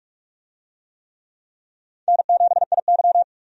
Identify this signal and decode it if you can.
Morse code, heard as a keyed tone, about 46 wpm, 700 Hz. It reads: N7IY
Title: USA Call Signs - 4 Characters - 45wpm